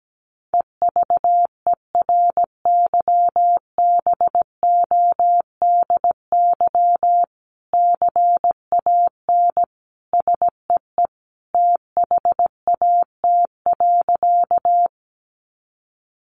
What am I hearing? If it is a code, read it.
Morse code, heard as a keyed tone, about 17 wpm, 700 Hz. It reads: EVERYBODY CAN SEE THAT.
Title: Sentences from Top 1400 Words 17wpm